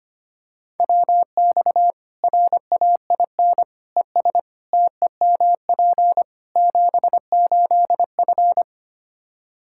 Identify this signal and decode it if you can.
Morse code, heard as a keyed tone, about 25 wpm, 700 Hz. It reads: WX RAIN ES TEMP 78F
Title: K1USN SST QSO Practice 25wpm